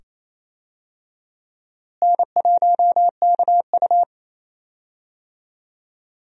Morse code, heard as a keyed tone, about 28 wpm, 700 Hz. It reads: N1KU
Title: USA Call Signs - 4 Characters - Repeated 2x - 28wpm